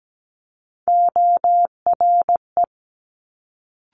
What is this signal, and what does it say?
Morse code, heard as a keyed tone, about 17 wpm, 700 Hz. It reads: ORE
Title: Multi-letter Phonemes Quick Fire 17wpm